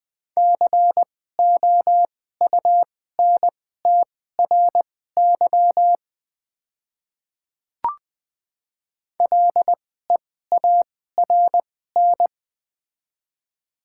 Morse code, heard as a keyed tone, about 20 wpm, spaced as for 14 wpm, 700 Hz. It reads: COUNTRY LEARN
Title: 2x Character Spacing - Top 300 Words - 20wpm